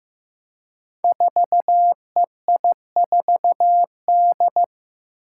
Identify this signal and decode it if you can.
Morse code, heard as a keyed tone, about 15 wpm, 700 Hz. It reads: 4EI4D
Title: Five Challenging Letter-Number Combinations 15wpm